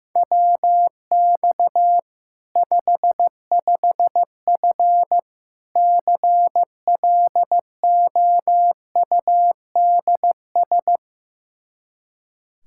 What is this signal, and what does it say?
Morse code, heard as a keyed tone, about 15 wpm, 700 Hz. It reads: WX 55F CLOUDS
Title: Practice QSO - Exchange 2 - Info - 15wpm